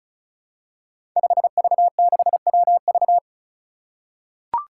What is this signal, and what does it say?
Morse code, heard as a keyed tone, about 36 wpm, 700 Hz. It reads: 5V6WV
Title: Five Challenging Letter-Number Combinations 35wpm